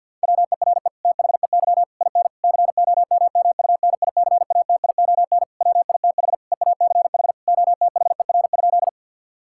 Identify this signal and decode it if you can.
Morse code, heard as a keyed tone, about 50 wpm, 700 Hz. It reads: WERE THEY IN COMMUNICATION WITH EACH OTHER?
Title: Sentences from Top 1000 Words - Review Part 6 - 50wpm